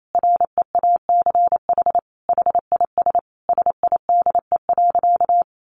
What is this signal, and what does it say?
Morse code, heard as a keyed tone, about 28 wpm, 700 Hz. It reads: REACH HIS SIDE.
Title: Sentences from Top 400 Words 28wpm